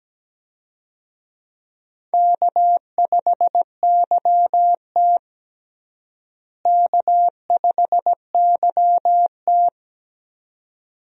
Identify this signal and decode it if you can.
Morse code, heard as a keyed tone, about 17 wpm, 700 Hz. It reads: K5YT K5YT
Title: USA Call Signs - 4 Characters - Repeated 2x - 17wpm